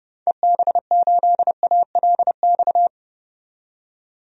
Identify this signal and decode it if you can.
Morse code, heard as a keyed tone, about 30 wpm, 700 Hz. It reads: EB8ALX
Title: DXCC-100 Call Signs - 6 Characters - 30wpm